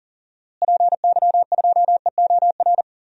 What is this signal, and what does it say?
Morse code, heard as a keyed tone, about 40 wpm, 700 Hz. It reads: PY2EOR
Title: DXCC-100 Call Signs - 6 Characters - Repeated 2x - 40wpm